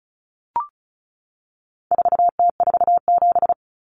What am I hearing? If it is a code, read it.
Morse code, heard as a keyed tone, about 34 wpm, 700 Hz. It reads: 4T47
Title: Common Cut Numbers - 4-Digit 35wpm